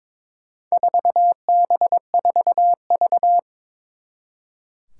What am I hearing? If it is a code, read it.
Morse code, heard as a keyed tone, about 22 wpm, 700 Hz. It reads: 4B4V
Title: Four Challenging Letter-Number Combinations 22wpm